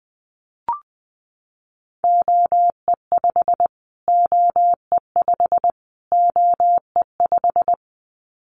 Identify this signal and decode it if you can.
Morse code, heard as a keyed tone, about 20 wpm, 700 Hz. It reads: OE5 OE5 OE5
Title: Callsign Prefixes for Top 25 DXCC - Repeated 3x - 20wpm